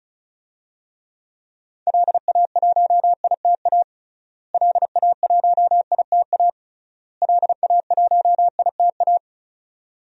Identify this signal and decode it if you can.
Morse code, heard as a keyed tone, about 36 wpm, 700 Hz. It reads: LA1ITA LA1ITA LA1ITA
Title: DXCC-100 Call Signs - 6 Characters - Repeated 3x - 35wpm